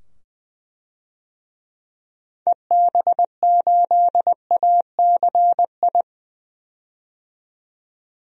Morse code, heard as a keyed tone, about 20 wpm, 700 Hz. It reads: EB8ACI
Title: DXCC-100 Call Signs - 6 Characters - 20wpm